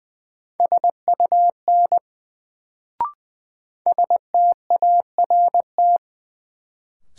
Morse code, heard as a keyed tone, about 21 wpm, 700 Hz. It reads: SUN START
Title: Sentences from Top 300 Words - New Word Drill - 20wpm